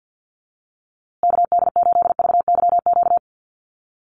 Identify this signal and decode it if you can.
Morse code, heard as a keyed tone, about 50 wpm, 700 Hz. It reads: KD8VYQ
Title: USA Call Signs Repeated 2x 50wpm